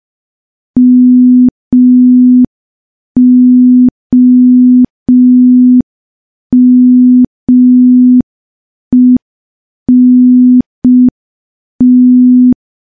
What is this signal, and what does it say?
Morse code, heard as a keyed tone, about 5 wpm, 250 Hz. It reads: MOMENT
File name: morse / morse-training-05WPM